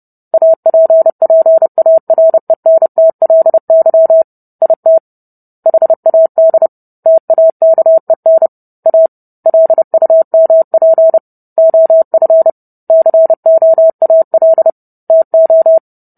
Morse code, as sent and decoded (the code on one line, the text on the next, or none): .- .--. .--. .- .-. -. - .-.. -.-- / .. - / .... .- -.. / - .- -.- . -. / .- / .-.. ..- -- .--. / --- ..-. / -.-. --- .- .-.. / - ---
APPARENTLY IT HAD TAKEN A LUMP OF COAL TO